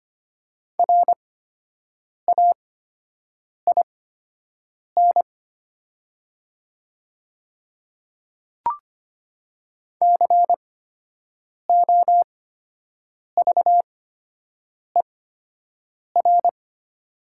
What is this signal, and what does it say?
Morse code, heard as a keyed tone, about 25 wpm, spaced as for 7 wpm, 700 Hz. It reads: RAIN COVER
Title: 8x Character Spacing - Top 300 Words - 25wpm